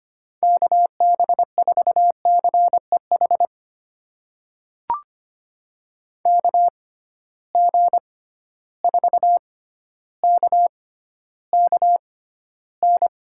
Morse code, heard as a keyed tone, about 25 wpm, spaced as for 13 wpm, 700 Hz. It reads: KB4CEH KG4KKN
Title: USA Call Signs - 6 Characters - 6x Character Spacing - 25wpm